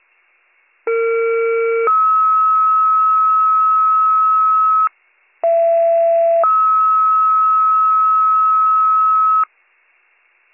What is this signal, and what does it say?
Two-tone page, heard as a one-second tone, then a three-second tone, about 460 then 1290 Hz, about 670 then 1290 Hz. It is heard from 0.9 to 9.4 s.